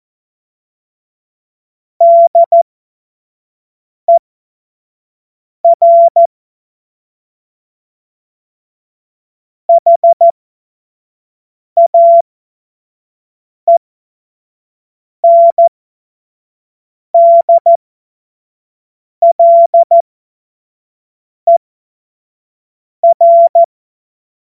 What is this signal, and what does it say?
Morse code, heard as a keyed tone, about 14 wpm, spaced as for 5 wpm, 680 Hz. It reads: DER HAENDLER
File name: Morsekurs / Witze / Witz 30 Januar 2024 25 F70